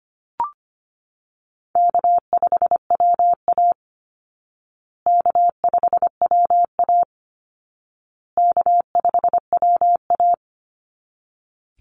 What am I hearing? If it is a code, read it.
Morse code, heard as a keyed tone, about 25 wpm, 700 Hz. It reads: K5WA K5WA K5WA
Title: Call Signs Repeated 2x - 25wpm